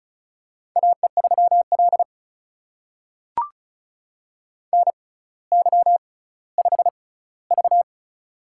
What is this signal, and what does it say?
Morse code, heard as a keyed tone, about 36 wpm, spaced as for 18 wpm, 700 Hz. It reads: AE3L NY5V
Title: USA Call Signs - 4 Characters - 6x Character Spacing - 35wpm